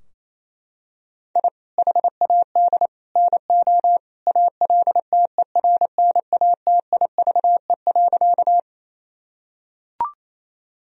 Morse code, heard as a keyed tone, about 28 wpm, 700 Hz. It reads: I HAD NO ALTERNATIVE.
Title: Sentences from Top 1000 Words - Review Part 4 - 28wpm